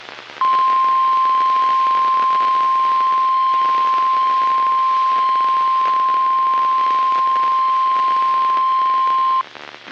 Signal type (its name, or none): Emergency Alert System (SAME) tones